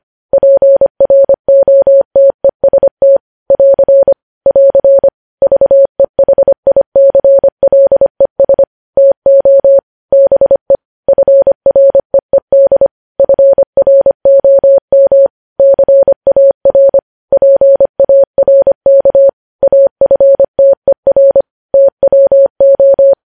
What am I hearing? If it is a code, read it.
Morse code, heard as a keyed tone, about 25 wpm, 550 Hz. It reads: PROTEST <AR> <AR> VEHICLES TO BE FREED FROM CAR PARK AFTER TWO
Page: News Headlines in Morse Code at 25 WPM